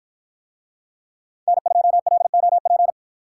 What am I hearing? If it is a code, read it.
Morse code, heard as a keyed tone, about 53 wpm, 700 Hz. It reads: N2LOP